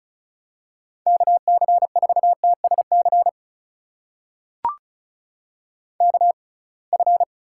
Morse code, heard as a keyed tone, about 36 wpm, spaced as for 27 wpm, 700 Hz. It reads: KC4TSC KF
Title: USA Call Signs - 6 Characters - 6x Character Spacing - 35wpm